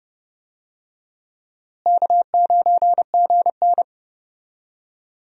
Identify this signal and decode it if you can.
Morse code, heard as a keyed tone, about 31 wpm, 700 Hz. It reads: K9GN
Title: USA Call Signs - 4 Characters - Repeated 3x - 30wpm